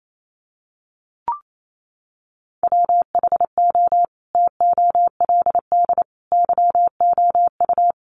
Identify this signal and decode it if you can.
Morse code, heard as a keyed tone, about 28 wpm, 700 Hz. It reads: WHO TOLD YOU
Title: Sentences from Top 400 Words 28wpm